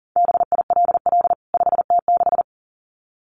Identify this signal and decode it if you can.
Morse code, heard as a keyed tone, about 40 wpm, 700 Hz. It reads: BILL 5T6
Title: CWT Exchange 40wpm